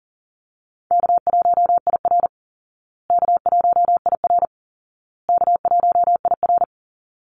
Morse code, heard as a keyed tone, about 40 wpm, 700 Hz. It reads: K1IR K1IR K1IR
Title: Easy Call Signs Repeated 3x - 40wpm